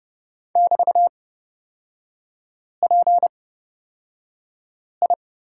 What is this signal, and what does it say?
Morse code, heard as a keyed tone, about 30 wpm, spaced as for 5 wpm, 700 Hz. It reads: <BT>PI
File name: Lesson 36e - BT - Three Characters 30wpm